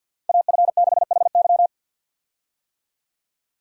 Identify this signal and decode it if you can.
Morse code, heard as a keyed tone, about 52 wpm, 700 Hz. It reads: AF6HY
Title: USA Call Signs - 5 Characters - 50wpm